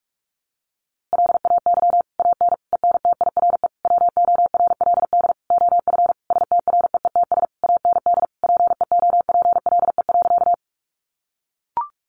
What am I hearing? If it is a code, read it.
Morse code, heard as a keyed tone, about 44 wpm, 700 Hz. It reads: LAY AN ENTIRE WORLD OF STREETS AND PEOPLE.
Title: A Sound Of Thunder 45wpm